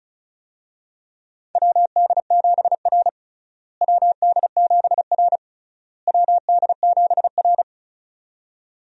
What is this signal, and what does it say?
Morse code, heard as a keyed tone, about 36 wpm, 700 Hz. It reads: WD7R WD7R WD7R
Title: USA Call Signs - 4 Characters - Repeated 3x - 35wpm